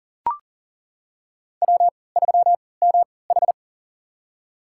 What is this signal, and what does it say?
Morse code, heard as a keyed tone, about 40 wpm, spaced as for 23 wpm, 700 Hz. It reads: W3MH
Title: USA Call Signs - 4 Characters - 3x Character Spacing - 40wpm